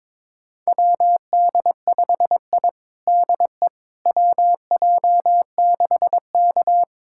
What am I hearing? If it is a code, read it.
Morse code, heard as a keyed tone, about 22 wpm, 700 Hz. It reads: WD5I DE WJ6K
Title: USA Callsign DE Callsign - 4 Characters - 22wpm